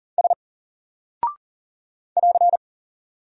Morse code, heard as a keyed tone, about 39 wpm, 700 Hz. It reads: S <AR>
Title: Single Character (Letter:Number:Punctuation:Prosign) 40wpm